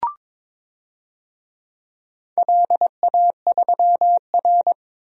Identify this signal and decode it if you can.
Morse code, heard as a keyed tone, about 22 wpm, 700 Hz. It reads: LA3R
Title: DXCC-100 Call Signs - 4 Characters - Repeated 3x - 22wpm